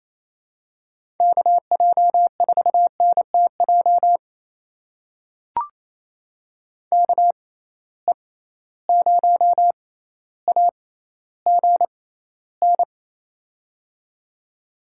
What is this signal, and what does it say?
Morse code, heard as a keyed tone, about 28 wpm, spaced as for 14 wpm, 700 Hz. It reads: KJ4NTJ KE0AGN
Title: USA Call Signs - 6 Characters - 6x Character Spacing - 28wpm